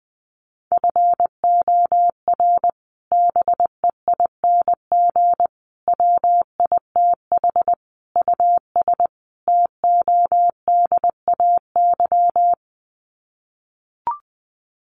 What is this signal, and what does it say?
Morse code, heard as a keyed tone, about 20 wpm, 700 Hz. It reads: FOR BEING WITH US TODAY
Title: Sets of 5 Words Encore2 20wpm